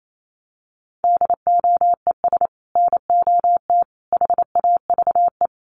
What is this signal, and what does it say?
Morse code, heard as a keyed tone, about 28 wpm, 700 Hz. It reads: DOES NOT HAVE
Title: Sets of 3 Words 28wpm